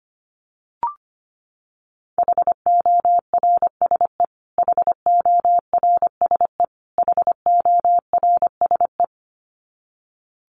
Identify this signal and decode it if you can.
Morse code, heard as a keyed tone, about 25 wpm, 700 Hz. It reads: HORSE HORSE HORSE
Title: Top 500 Words Repeated 25wpm